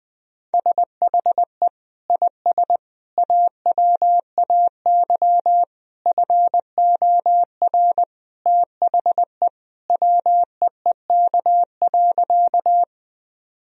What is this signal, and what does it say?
Morse code, heard as a keyed tone, about 21 wpm, 700 Hz. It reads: SHE IS AWAY FOR THE WEEK.
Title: Sentences from Top 1000 Words - Review Part 2 - 20wpm